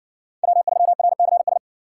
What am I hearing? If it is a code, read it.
Morse code, heard as a keyed tone, about 60 wpm, 700 Hz. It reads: W3RCS